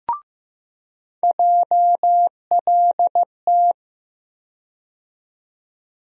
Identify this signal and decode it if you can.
Morse code, heard as a keyed tone, about 15 wpm, 700 Hz. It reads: JLT
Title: USA Call Sign Postfixes - 3 Letters 15wpm